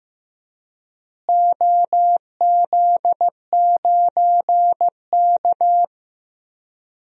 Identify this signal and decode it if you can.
Morse code, heard as a keyed tone, about 15 wpm, 700 Hz. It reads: OZ9K
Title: DXCC-100 Call Signs - 4 Characters - Repeated 3x - 15wpm